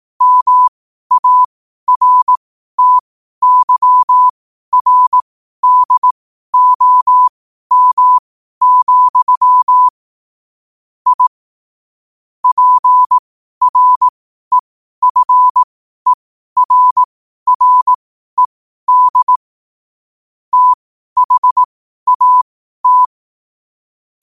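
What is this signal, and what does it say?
Morse code, heard as a keyed tone, about 18 wpm, spaced as for 12 wpm, 1000 Hz. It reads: MARTYRDOM, I PREFERRED THAT
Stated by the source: Morse code at 12 wpm.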